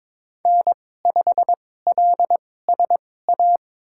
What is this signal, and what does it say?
Morse code, heard as a keyed tone, about 22 wpm, spaced as for 16 wpm, 700 Hz. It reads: N5LSA
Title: USA Call Signs - 2x Character Spacing - 22wpm